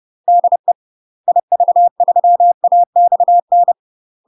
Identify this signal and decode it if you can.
Morse code, heard as a keyed tone, about 30 wpm, 700 Hz. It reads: DE IV3AXN
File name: UK MASTER QSOs TRAINING / 30WPM / UK MASTER QSOs 0024